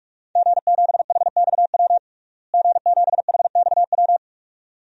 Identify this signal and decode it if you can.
Morse code, heard as a keyed tone, about 42 wpm, 700 Hz. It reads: G7HXW G7HXW
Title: DXCC-100 Call Signs - 5 Characters - Repeated 2x - 45wpm